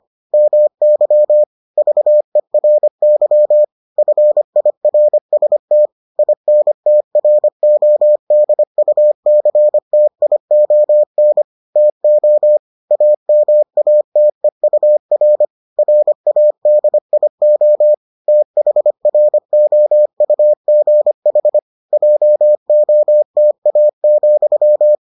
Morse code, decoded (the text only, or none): MY VERY FIRST INTRODUCTION TO AMATEUR RADIO THROUGH JOTA,